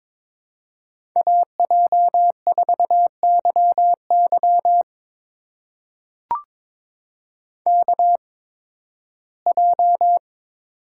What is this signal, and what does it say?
Morse code, heard as a keyed tone, about 22 wpm, spaced as for 15 wpm, 700 Hz. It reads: AJ4YY KJ
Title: USA Call Signs - 5 Characters - 8x Character Spacing - 22wpm